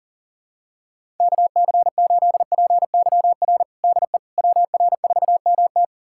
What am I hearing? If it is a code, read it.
Morse code, heard as a keyed tone, about 37 wpm, 700 Hz. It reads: KC8PYR DE WR4MT
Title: USA Callsign DE Callsign 40wpm